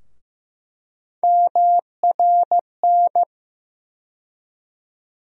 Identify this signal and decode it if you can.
Morse code, heard as a keyed tone, about 15 wpm, 700 Hz. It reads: MRN
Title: USA Call Sign Postfixes - 3 Letters 15wpm